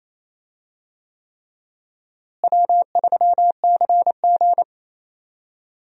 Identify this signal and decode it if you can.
Morse code, heard as a keyed tone, about 28 wpm, 700 Hz. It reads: W3CG
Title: USA Call Signs - 4 Characters - Repeated 3x - 28wpm